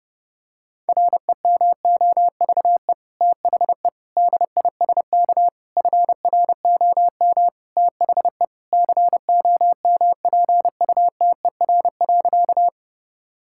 Morse code, heard as a keyed tone, about 30 wpm, 700 Hz. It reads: REMOVE THE DISK FROM THE COMPUTER.
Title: Sentences from Top 1000 Words - Review Part 6 - 30wpm